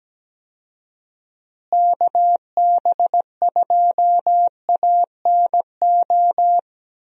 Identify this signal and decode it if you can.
Morse code, heard as a keyed tone, about 17 wpm, 700 Hz. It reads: KB2ANO
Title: USA Call Signs - 6 Characters - Repeated 3x - 17wpm